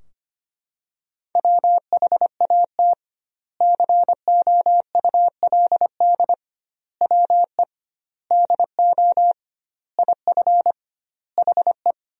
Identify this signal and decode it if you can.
Morse code, heard as a keyed tone, about 25 wpm, 700 Hz. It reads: WHAT COULD WE DO IF HE
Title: 2x Word-Spacing - Sentences from Top 400 Words - 25wpm